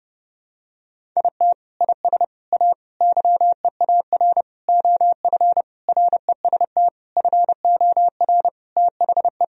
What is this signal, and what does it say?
Morse code, heard as a keyed tone, about 30 wpm, 700 Hz. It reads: IT IS A YEAR OF REST FOR THE
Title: Sentences from Top 1000 Words - Review Part 1 - 30wpm